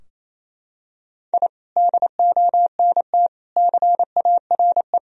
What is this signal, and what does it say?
Morse code, heard as a keyed tone, about 28 wpm, 700 Hz. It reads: I DONT CARE
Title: Sentences from Top 1000 Words - Review Part 4 - 28wpm